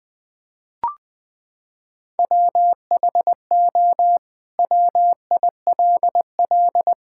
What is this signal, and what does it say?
Morse code, heard as a keyed tone, about 20 wpm, 700 Hz. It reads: WHO WILL
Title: Sentences from Top 1000 Words - Review Part 5 - 20wpm